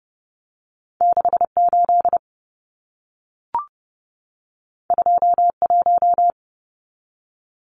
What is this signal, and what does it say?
Morse code, heard as a keyed tone, about 30 wpm, 700 Hz. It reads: 68 21
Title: Numbers - 2-Digit 30wpm